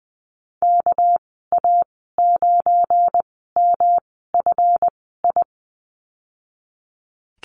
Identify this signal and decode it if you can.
Morse code, heard as a keyed tone, about 20 wpm, spaced as for 15 wpm, 700 Hz. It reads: KA9MFI
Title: 200 Lesson 32g - Letter k - Call Signs 20wpm